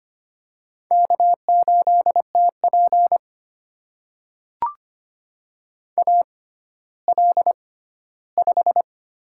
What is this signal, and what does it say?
Morse code, heard as a keyed tone, about 25 wpm, spaced as for 14 wpm, 700 Hz. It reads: K8TP AL5
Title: USA Call Signs - 6x Character Spacing - 25wpm